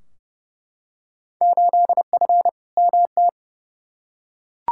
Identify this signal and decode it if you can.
Morse code, heard as a keyed tone, about 29 wpm, 700 Hz. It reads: 8F MT
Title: Field Day Exchanges Encore - 30wpm